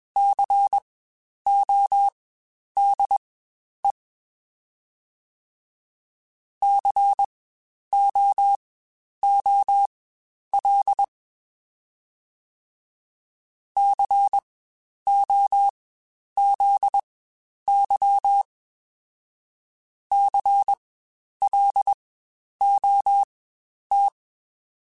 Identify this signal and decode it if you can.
Morse code, heard as a keyed tone, about 20 wpm, spaced as for 10 wpm, 800 Hz. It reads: CODE COOL COZY CLOT
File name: Spaced Alphas / Spaced Alphas mp3s / Cs sp @20